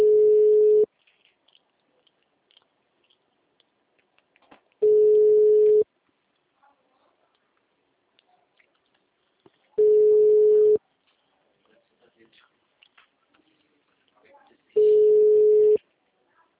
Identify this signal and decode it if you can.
Ringback tone (Europe, 425 Hz), with speech.